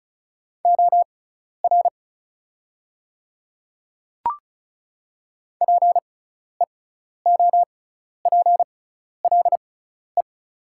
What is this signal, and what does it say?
Morse code, heard as a keyed tone, about 36 wpm, spaced as for 12 wpm, 690 Hz. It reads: OR PEOPLE
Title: 6x Character Spacing - Top 100 Words - 35wpm